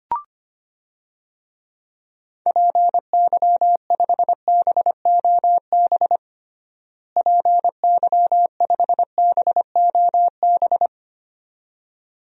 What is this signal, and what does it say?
Morse code, heard as a keyed tone, about 25 wpm, 700 Hz. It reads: PY5BOB PY5BOB
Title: DXCC-100 Call Signs - 6 Characters - Repeated 2x - 25wpm